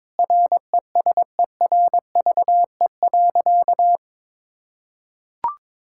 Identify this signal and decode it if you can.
Morse code, heard as a keyed tone, about 22 wpm, 700 Hz. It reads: RESERVE.
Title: Sentences from Top 1000 Words - Review Part 6 - 22wpm